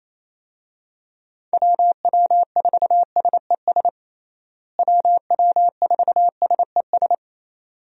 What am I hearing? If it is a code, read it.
Morse code, heard as a keyed tone, about 28 wpm, 700 Hz. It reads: WW4SES WW4SES
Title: USA Call Signs - 6 Characters - Repeated 2x - 28wpm